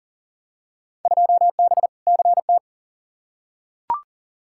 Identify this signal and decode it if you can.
Morse code, heard as a keyed tone, about 40 wpm, 700 Hz. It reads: 2B CT
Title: Field Day Exchanges Encore - 40wpm